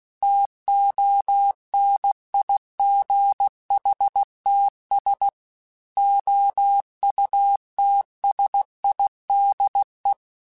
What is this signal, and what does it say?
Morse code, heard as a keyed tone, about 16 wpm, 800 Hz. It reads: TONIGHTS OUTSIDE
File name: Weather Sets / Weather Sets-mp3s / Weather 4 24 @15